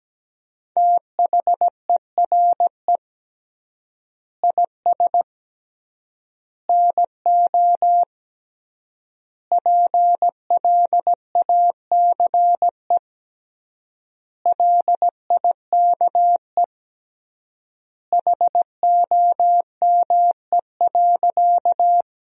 Morse code, heard as a keyed tone, about 17 wpm, 700 Hz. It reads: THERE IS NO PLACE LIKE HOME.
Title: 3x Word-Spacing - Sentences from Top 200 Words - 17wpm